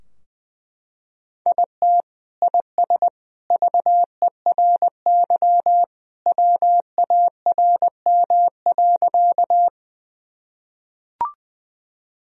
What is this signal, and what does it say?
Morse code, heard as a keyed tone, about 20 wpm, 700 Hz. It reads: IT IS VERY WARM.
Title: Sentences from Top 1000 Words - Review Part 2 - 20wpm